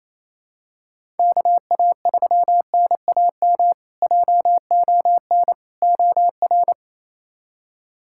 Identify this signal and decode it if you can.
Morse code, heard as a keyed tone, about 28 wpm, 700 Hz. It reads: KA3NAM JON OR
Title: K1USN SST QSO Practice - 28wpm